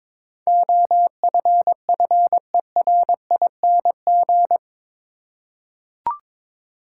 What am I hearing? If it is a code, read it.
Morse code, heard as a keyed tone, about 23 wpm, 700 Hz. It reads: OFFERING
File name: practice / 22wpm / Eight Letter Words - 22wpm